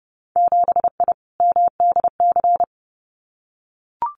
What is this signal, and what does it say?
Morse code, heard as a keyed tone, about 30 wpm, 700 Hz. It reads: 7I MDC
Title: Winter Field Day Exchanges Encore 30wpm